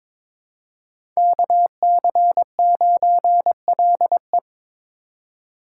Morse code, heard as a keyed tone, about 21 wpm, 700 Hz. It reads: KC9LE
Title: USA Call Signs - 5 Characters - 2x Character Spacing - 22wpm